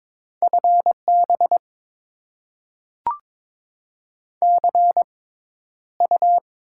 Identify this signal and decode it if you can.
Morse code, heard as a keyed tone, about 22 wpm, spaced as for 11 wpm, 700 Hz. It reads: FB CU
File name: Top 20 QSO Elements - 6x Character Spacing - 22wpm